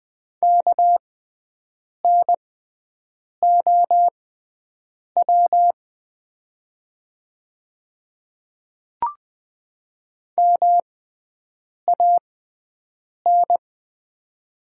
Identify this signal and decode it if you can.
Morse code, heard as a keyed tone, about 20 wpm, spaced as for 7 wpm, 700 Hz. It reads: KNOW MAN
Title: 6x Character Spacing - Top 300 Words - 20wpm